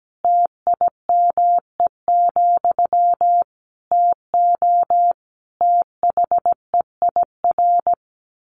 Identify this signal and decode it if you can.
Morse code, heard as a keyed tone, about 17 wpm, 700 Hz. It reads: TIME, TO THEIR
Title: A Sound Of Thunder 17wpm